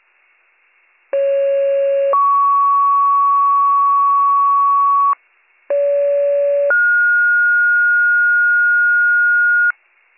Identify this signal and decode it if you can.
Two-tone page, heard as a one-second tone, then a three-second tone, about 550 then 1120 Hz, about 570 then 1430 Hz. It is heard from 1.1 to 9.7 s.